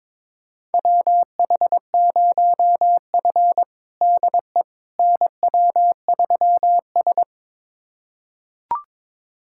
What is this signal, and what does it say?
Morse code, heard as a keyed tone, about 22 wpm, 700 Hz. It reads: WH0F DE NW3S
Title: USA Callsign DE Callsign - 4 Characters - 22wpm